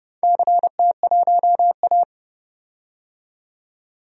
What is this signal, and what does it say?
Morse code, heard as a keyed tone, about 30 wpm, 700 Hz. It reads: CT1A
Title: DXCC-100 Call Signs - 4 Characters - 30wpm